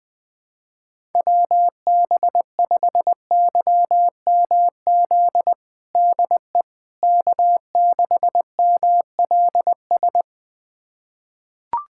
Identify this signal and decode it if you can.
Morse code, heard as a keyed tone, about 20 wpm, 700 Hz. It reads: WB5YMZ DE K6MLS
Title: USA Callsign DE Callsign 20wpm